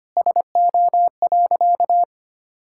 Morse code, heard as a keyed tone, about 25 wpm, 700 Hz. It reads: SO.
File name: practice / 25wpm / Sentences with 4-Word Phrases - 25wpm